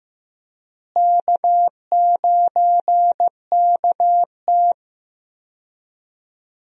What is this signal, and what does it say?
Morse code, heard as a keyed tone, about 15 wpm, 700 Hz. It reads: K9KT
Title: USA Call Signs - 4 Characters - Repeated 2x - 15wpm